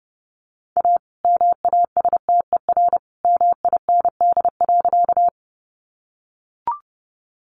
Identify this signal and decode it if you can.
Morse code, heard as a keyed tone, about 30 wpm, 700 Hz. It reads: A MASTER MIND.
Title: Sentences from Top 1300 Words 30wpm